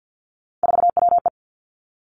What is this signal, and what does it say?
Morse code, heard as a keyed tone, about 50 wpm, 700 Hz. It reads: 4WE
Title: Three Challenging Letter-Number Combinations 50wpm